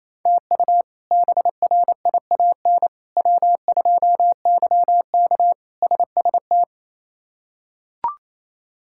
Morse code, heard as a keyed tone, about 28 wpm, 700 Hz. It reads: TU BRIAN W2YK SST
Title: K1USN SST QSO Practice - 28wpm